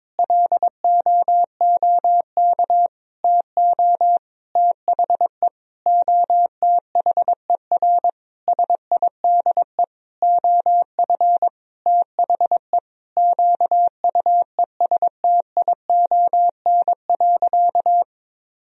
Morse code, heard as a keyed tone, about 21 wpm, 700 Hz. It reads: LOOK TO THE OTHER SIDE OF THE QUESTION.
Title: Sentences from Top 1000 Words - Review Part 1 - 22wpm